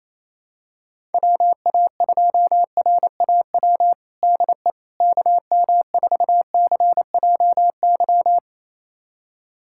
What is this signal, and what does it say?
Morse code, heard as a keyed tone, about 27 wpm, 700 Hz. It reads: WA2RAW DE KM4CJY
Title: USA Callsign DE Callsign - 6 Characters - 28wpm